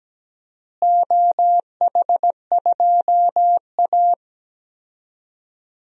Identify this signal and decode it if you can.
Morse code, heard as a keyed tone, about 17 wpm, 700 Hz. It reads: OH2A
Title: DXCC-100 Call Signs - 4 Characters - Repeated 3x - 17wpm